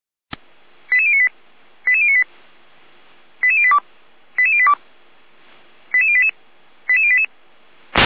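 Five-tone selective call (ZVEI), heard as quick runs of five tones, tones of about 70 ms. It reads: 88098 88098 88082 88082 88088 88088
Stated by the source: ZVEI five-tone selective call 88571 received by SDR.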